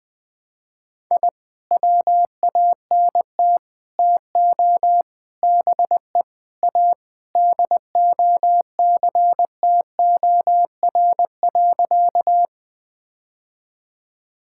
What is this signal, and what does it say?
Morse code, heard as a keyed tone, about 20 wpm, 700 Hz. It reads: I WANT TO BE A DOCTOR.
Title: Sentences with 3-Word Phrases - 20wpm